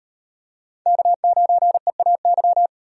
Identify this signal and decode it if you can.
Morse code, heard as a keyed tone, about 38 wpm, 700 Hz. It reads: K9EAY